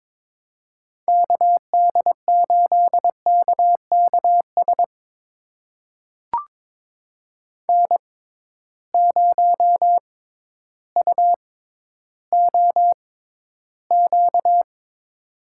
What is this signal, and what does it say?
Morse code, heard as a keyed tone, about 22 wpm, spaced as for 12 wpm, 700 Hz. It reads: KD8KKS N0UOQ
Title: USA Call Signs - 6x Character Spacing - 22wpm